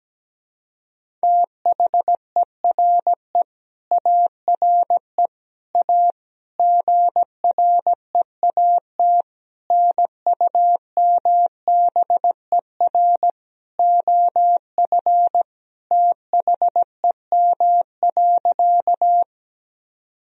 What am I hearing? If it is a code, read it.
Morse code, heard as a keyed tone, about 17 wpm, 700 Hz. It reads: THERE ARE A GREAT NUMBER OF THEM.
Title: Sentences from Top 1000 Words - Review Part 1 - 17wpm